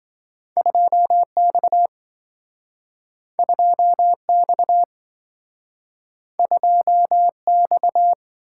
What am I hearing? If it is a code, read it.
Morse code, heard as a keyed tone, about 27 wpm, 700 Hz. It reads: 2X 2X 2X